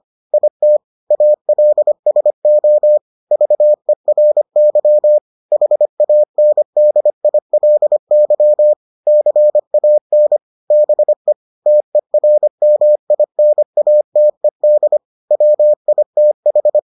Morse code, heard as a keyed tone, about 25 wpm, 600 Hz. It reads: IT ALSO VERY HANDILY CAN BE TERMINATED WITH